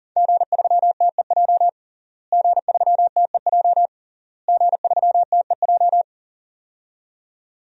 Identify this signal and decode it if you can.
Morse code, heard as a keyed tone, about 37 wpm, 700 Hz. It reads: G3TEJ G3TEJ G3TEJ
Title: DXCC-100 Call Signs - 5 Characters - Repeated 3x - 40wpm